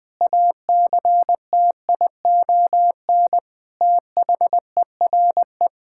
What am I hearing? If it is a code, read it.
Morse code, heard as a keyed tone, about 19 wpm, 700 Hz. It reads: ACTION THERE
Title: Sentences with 8-Word Phrases - Spoken Beginning - 20wpm